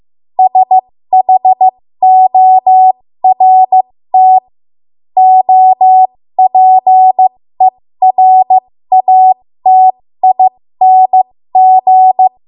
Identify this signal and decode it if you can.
Morse code, heard as a keyed tone, about 15 wpm, spaced as for 13 wpm, 750 Hz. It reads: SHORT OPERATING
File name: Morse / 13WPM